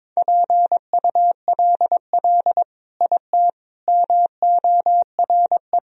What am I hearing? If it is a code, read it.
Morse code, heard as a keyed tone, about 22 wpm, 700 Hz. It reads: PULL IT MORE
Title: Sentences from Top 400 Words and Abbreviations 22wpm